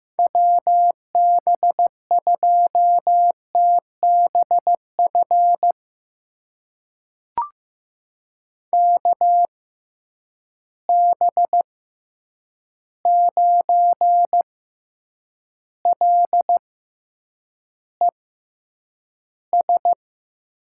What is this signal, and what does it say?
Morse code, heard as a keyed tone, about 15 wpm, spaced as for 8 wpm, 700 Hz. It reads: WB2TBF KB9LES
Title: USA Call Signs - 6 Characters - 6x Character Spacing - 15wpm